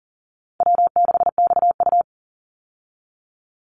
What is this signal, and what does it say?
Morse code, heard as a keyed tone, about 40 wpm, 700 Hz. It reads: W6XU
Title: Easy Call Signs Repeated 3x - 40wpm